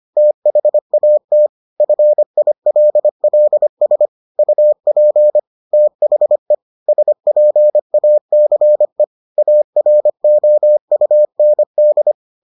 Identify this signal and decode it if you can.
Morse code, heard as a keyed tone, about 25 wpm, 600 Hz. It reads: THAT FILLS UP THE SPACE AROUND